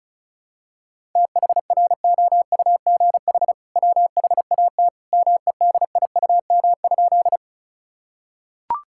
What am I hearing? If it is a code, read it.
Morse code, heard as a keyed tone, about 36 wpm, 700 Hz. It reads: THROUGH WHAT MEDIUM?
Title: Sentences from Top 1000 Words - Review Part 6 - 35wpm